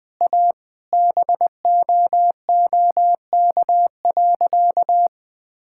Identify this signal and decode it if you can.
Morse code, heard as a keyed tone, about 20 wpm, 700 Hz. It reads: A BOOK.
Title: Sentences with 4-Word Phrases - Spoken Beginning - 20wpm